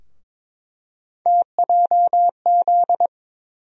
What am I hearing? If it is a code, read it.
Morse code, heard as a keyed tone, about 22 wpm, 700 Hz. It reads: TJZ
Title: USA Call Sign Postfixes - 3 Letters 22wpm